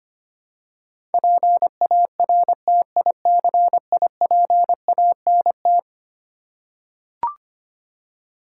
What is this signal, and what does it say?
Morse code, heard as a keyed tone, about 25 wpm, 700 Hz. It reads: PARTICIPANT
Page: morse-code-ninja-practice-25wpm